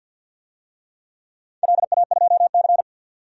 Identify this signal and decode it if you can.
Morse code, heard as a keyed tone, about 52 wpm, 700 Hz. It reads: LA2C
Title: DXCC-100 Call Signs - 4 Characters - 50wpm